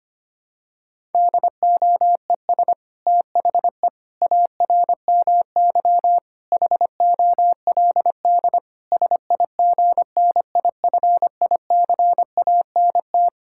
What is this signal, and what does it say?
Morse code, heard as a keyed tone, about 25 wpm, 700 Hz. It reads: DOES THE ARMY HOLD SIGNIFICANT
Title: Sentences from Top 1000 Words - Review Part 6 - 25wpm